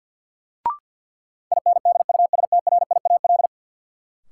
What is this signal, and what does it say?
Morse code, heard as a keyed tone, about 53 wpm, 700 Hz. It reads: INDUSTRIAL
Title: Sets of 5 Words - New Word Drill - 50wpm